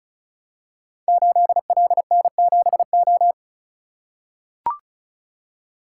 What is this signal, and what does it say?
Morse code, heard as a keyed tone, about 36 wpm, 690 Hz. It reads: 8LN7O
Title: Code Groups - 5 Letter:Numbers - 35wpm